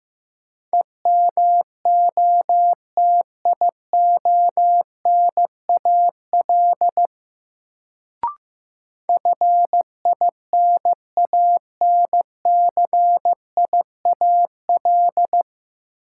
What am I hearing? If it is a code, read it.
Morse code, heard as a keyed tone, about 15 wpm, 700 Hz. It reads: EMOTIONAL FINANCIAL
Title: Sentences from Top 1000 Words - Review Part 5 - New Word Drill - 15wpm